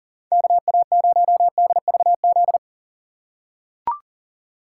Morse code, heard as a keyed tone, about 40 wpm, 700 Hz. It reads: KA0DVZ
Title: USA Call Signs - 6 Characters - 3x Character Spacing - 40wpm